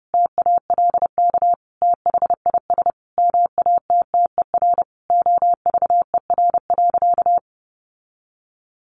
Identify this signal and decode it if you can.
Morse code, heard as a keyed tone, about 30 wpm, 700 Hz. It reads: TALK THIS MATTER OVER.
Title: Sentences from Top 700 Words 30wpm